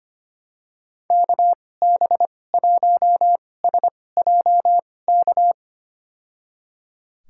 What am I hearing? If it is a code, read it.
Morse code, heard as a keyed tone, about 25 wpm, spaced as for 18 wpm, 700 Hz. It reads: KB1SJK
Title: USA Call Signs - 6 Characters - 2x Character Spacing - 25wpm